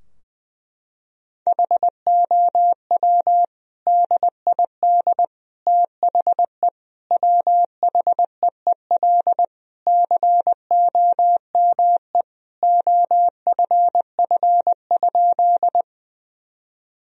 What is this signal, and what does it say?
Morse code, heard as a keyed tone, about 20 wpm, 700 Hz. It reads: HOW DID THE WHEEL COME OFF?
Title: Sentences from Top 1000 Words - Review Part 3 - 20wpm